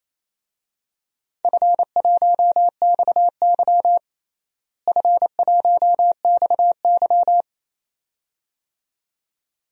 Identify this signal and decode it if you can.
Morse code, heard as a keyed tone, about 28 wpm, 700 Hz. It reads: F1XY F1XY
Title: DXCC-100 Call Signs - 4 Characters - Repeated 2x - 28wpm